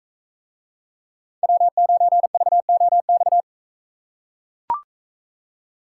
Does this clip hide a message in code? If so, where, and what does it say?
Yes: Morse code from 1.4 to 3.4 s, heard as a keyed tone, about 42 wpm, 700 Hz. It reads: W9VOX